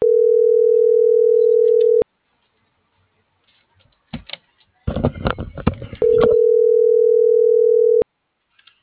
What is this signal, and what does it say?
Ringback tone (North America), 440 + 480 Hz.
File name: CENTRAL TERRESTRE / out cris.f. - .49493